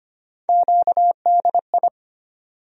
Morse code, heard as a keyed tone, about 25 wpm, 700 Hz. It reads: QDI
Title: USA Call Sign Postfixes - 3 Letters 25wpm